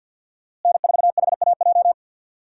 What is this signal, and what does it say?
Morse code, heard as a keyed tone, about 52 wpm, 700 Hz. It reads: N4HAJ
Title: USA Call Signs - 5 Characters - Repeated 3x - 50wpm